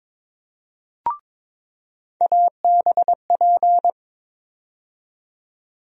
Morse code, heard as a keyed tone, about 22 wpm, 700 Hz. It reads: ABP
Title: USA Call Sign Postfixes - 3 Letters 22wpm